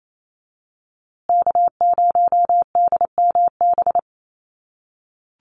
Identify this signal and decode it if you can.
Morse code, heard as a keyed tone, about 28 wpm, 700 Hz. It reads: K0DMB
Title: USA Call Signs Repeated 2x 28wpm